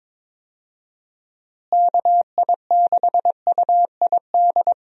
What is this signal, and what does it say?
Morse code, heard as a keyed tone, about 22 wpm, 700 Hz. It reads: KI6UID
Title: USA Call Signs - 6 Characters - Repeated 2x - 22wpm